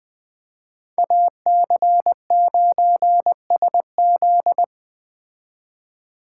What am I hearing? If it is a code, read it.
Morse code, heard as a keyed tone, about 20 wpm, 700 Hz. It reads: AC9SZ
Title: Speed Racing - USA 2x2 Call Signs 17wpm